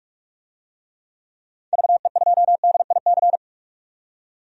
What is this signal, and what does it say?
Morse code, heard as a keyed tone, about 45 wpm, 700 Hz. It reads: VE2DIC
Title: DXCC-100 Call Signs - 6 Characters - 45wpm